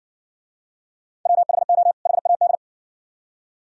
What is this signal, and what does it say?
Morse code, heard as a keyed tone, about 58 wpm, 700 Hz. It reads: WHO HAD